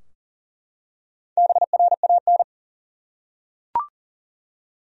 Morse code, heard as a keyed tone, about 40 wpm, 700 Hz. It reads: BRAN
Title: USA Common Names - Max 5 Letters - 40wpm